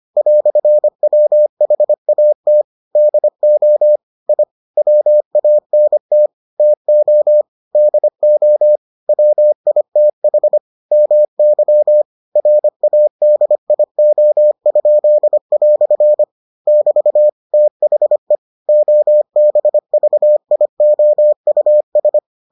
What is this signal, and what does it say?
Morse code, heard as a keyed tone, about 25 wpm, 600 Hz. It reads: "WHAT DO I WANT TO DO WITH MY RADIO?" <BT> THE OBVIOUS